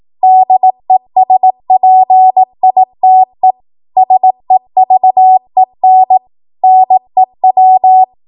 Morse code, heard as a keyed tone, about 18 wpm, 750 Hz. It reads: DESPITE SEVEN NEW
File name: Archive / 18 WPM / 18WPM